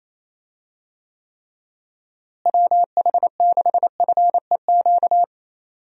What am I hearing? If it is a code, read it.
Morse code, heard as a keyed tone, about 28 wpm, 700 Hz. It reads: WH6FEQ
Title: USA Call Signs - 6 Characters - Repeated 2x - 28wpm